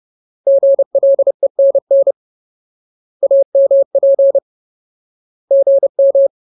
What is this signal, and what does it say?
Morse code, heard as a keyed tone, about 30 wpm, 550 Hz. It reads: GLENN AMP GM
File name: Ultimate Word List / 30WPM Ultimate